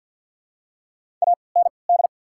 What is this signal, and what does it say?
Morse code, heard as a keyed tone, about 48 wpm, spaced as for 28 wpm, 700 Hz. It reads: AND
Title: 3x Character Spacing - Top 200 Words - 50wpm